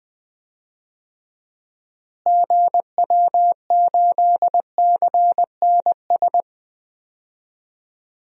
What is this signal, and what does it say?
Morse code, heard as a keyed tone, about 20 wpm, 700 Hz. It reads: GW8CNS
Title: DXCC-100 Call Signs - 6 Characters - Repeated 2x - 20wpm